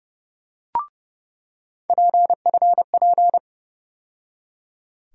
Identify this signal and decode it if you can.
Morse code, heard as a keyed tone, about 30 wpm, 700 Hz. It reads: PFP
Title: USA Call Sign Postfixes - 3 Letters 30wpm